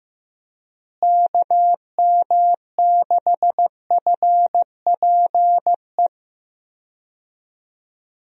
Morse code, heard as a keyed tone, about 15 wpm, 700 Hz. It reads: KM6FPE
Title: USA Call Signs - 6 Characters - Repeated 2x - 15wpm